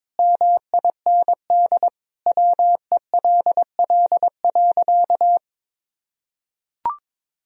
Morse code, heard as a keyed tone, about 22 wpm, 700 Hz. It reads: MIND WELL.
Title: Sentences from Top 1000 Words - Review Part 2 - 22wpm